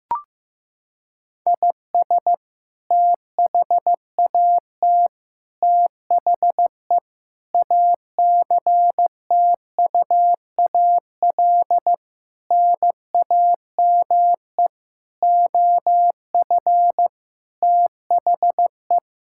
Morse code, heard as a keyed tone, about 15 wpm, 700 Hz. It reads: IS THAT THE ACTUAL NAME OF THE
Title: Sentences from Top 1000 Words - Review Part 5 - 15wpm